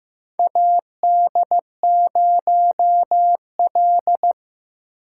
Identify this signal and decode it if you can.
Morse code, heard as a keyed tone, about 15 wpm, 700 Hz. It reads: AD0L
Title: USA Call Signs - 4 Characters - 2x Character Spacing - 15wpm